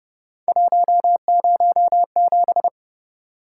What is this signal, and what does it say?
Morse code, heard as a keyed tone, about 29 wpm, 700 Hz. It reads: 107
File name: Speed-Racing - Three Letter-Number Combinations - 25wpm